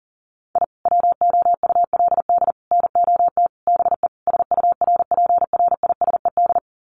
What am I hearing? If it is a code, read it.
Morse code, heard as a keyed tone, about 37 wpm, 700 Hz. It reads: I WOULD NOT BE SURPRISED
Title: Sets of 5 Words Encore2 40wpm